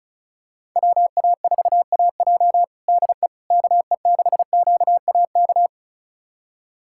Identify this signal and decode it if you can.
Morse code, heard as a keyed tone, about 34 wpm, 700 Hz. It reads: WA4AJ DE KE6QAK
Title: USA Callsign DE Callsign 35wpm